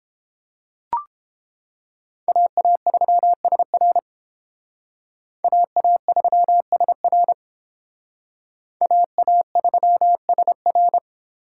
Morse code, heard as a keyed tone, about 30 wpm, 700 Hz. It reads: AA3SR AA3SR AA3SR